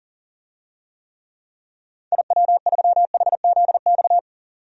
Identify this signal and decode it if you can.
Morse code, heard as a keyed tone, about 40 wpm, 700 Hz. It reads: IW3HZX
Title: DXCC-100 Call Signs - 6 Characters - 40wpm